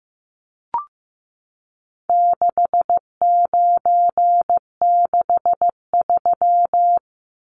Morse code, heard as a keyed tone, about 15 wpm, 700 Hz. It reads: 6963
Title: Numbers - 4-Digit 15wpm